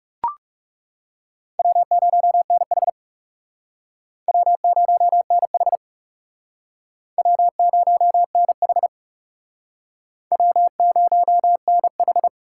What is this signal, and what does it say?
Morse code, heard as a keyed tone, about 40 wpm, 700 Hz. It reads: W0NH W0NH W0NH W0NH